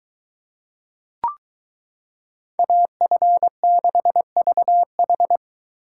Morse code, heard as a keyed tone, about 22 wpm, 700 Hz. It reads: AF6VH